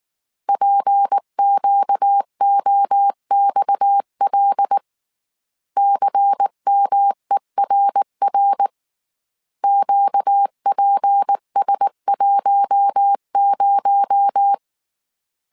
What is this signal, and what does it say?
Morse code, heard as a keyed tone, about 19 wpm, 800 Hz. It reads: PQOXL CMERR QPS10
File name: CW / DET / 1 6 1500C S A 18wpm 2255